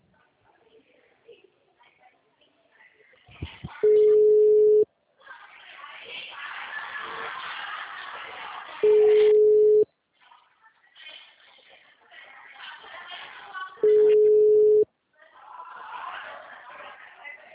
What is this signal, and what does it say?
Ringback tone (Europe, 425 Hz), with speech.
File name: out- -2021- - - .35561